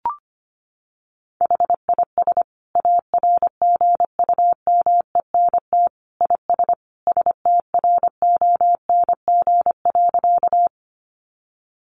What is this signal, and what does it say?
Morse code, heard as a keyed tone, about 25 wpm, 700 Hz. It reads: HIS ARGUMENT IS STRONG.
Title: Sentences from Top 1400 Words 25wpm